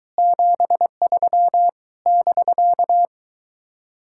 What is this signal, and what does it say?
Morse code, heard as a keyed tone, about 23 wpm, 700 Hz. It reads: 73 <BK>